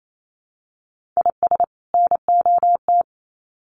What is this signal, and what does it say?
Morse code, heard as a keyed tone, about 28 wpm, 700 Hz. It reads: IS NOT
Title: Sets of 2 Words 28wpm